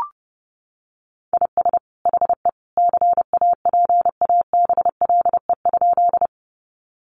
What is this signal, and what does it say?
Morse code, heard as a keyed tone, about 30 wpm, 700 Hz. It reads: IS HE CAPABLE?
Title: Sentences from Top 1200 Words 30wpm